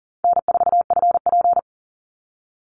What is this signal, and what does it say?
Morse code, heard as a keyed tone, about 40 wpm, 700 Hz. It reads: N4FP
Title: Call Signs Repeated 2x - 40wpm